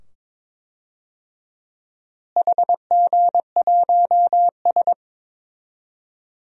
Morse code, heard as a keyed tone, about 22 wpm, 700 Hz. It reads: HG1S
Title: DXCC-100 Call Signs - 4 Characters - Repeated 2x - 22wpm